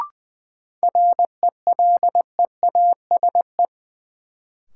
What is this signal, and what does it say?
Morse code, heard as a keyed tone, about 20 wpm, 700 Hz. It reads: RELEASE
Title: Sentences from Top 1100 Words - New Word Drill Alt - 20wpm